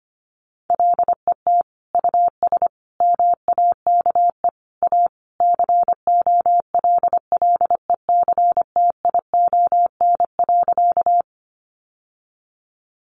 Morse code, heard as a keyed tone, about 25 wpm, 700 Hz. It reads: LET US MAKE A COLLECTION.
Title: Sentences from Top 1200 Words 25wpm